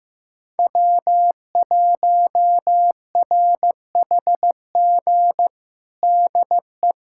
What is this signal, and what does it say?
Morse code, heard as a keyed tone, about 16 wpm, 700 Hz. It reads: W1RHG DE
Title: USA Callsign DE Callsign - 5 Characters - 15wpm